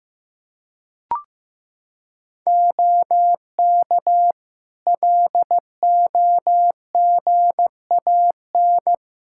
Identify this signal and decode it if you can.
Morse code, heard as a keyed tone, about 16 wpm, 700 Hz. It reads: OK LOGAN
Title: Practice QSO - Exchange 2 - Info - 15wpm